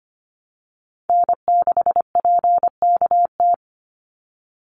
Morse code, heard as a keyed tone, about 25 wpm, 700 Hz. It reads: N6PKT
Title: NA SOTA Chaser Callsigns 25wpm